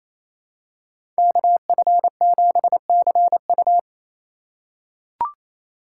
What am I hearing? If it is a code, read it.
Morse code, heard as a keyed tone, about 28 wpm, 700 Hz. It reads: KF7CU
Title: USA Call Signs - 5 Characters - 2x Character Spacing - 28wpm